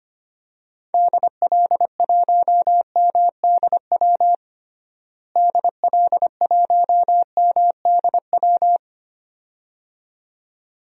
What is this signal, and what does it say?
Morse code, heard as a keyed tone, about 25 wpm, 700 Hz. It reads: DL1MDW DL1MDW
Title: DXCC-100 Call Signs - 6 Characters - Repeated 2x - 25wpm